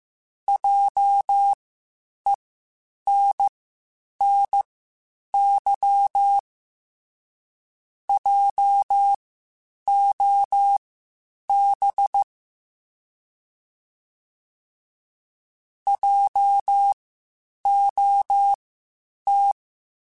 Morse code, heard as a keyed tone, about 15 wpm, spaced as for 8 wpm, 800 Hz. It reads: JENNY JOB JOT
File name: Spaced Alphas / Spaced Alphas mp3s / Js sp @13